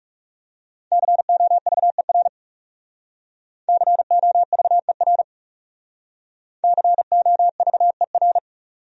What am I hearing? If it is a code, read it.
Morse code, heard as a keyed tone, about 38 wpm, 700 Hz. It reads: COVER COVER COVER